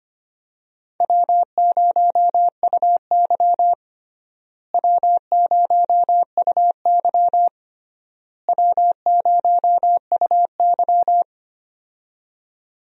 Morse code, heard as a keyed tone, about 25 wpm, 700 Hz. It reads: W0UY W0UY W0UY
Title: USA Call Signs - 4 Characters - Repeated 3x - 25wpm